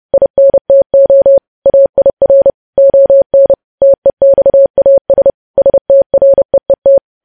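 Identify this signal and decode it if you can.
Morse code, heard as a keyed tone, about 30 wpm, 550 Hz. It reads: INTO AIR ON TEXAS STREET
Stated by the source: News headlines sent in Morse at 30.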